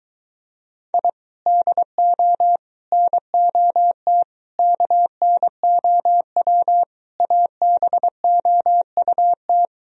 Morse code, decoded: I DO NOT KNOW ABOUT

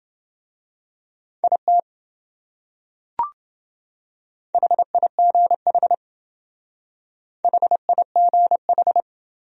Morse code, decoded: IT HIGH HIGH

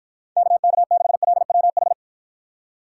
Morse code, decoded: KK6LWH

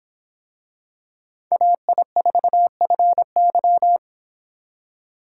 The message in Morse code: AI4FY

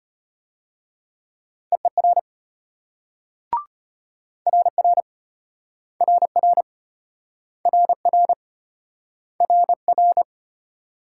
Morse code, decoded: EER RR RR RR RR

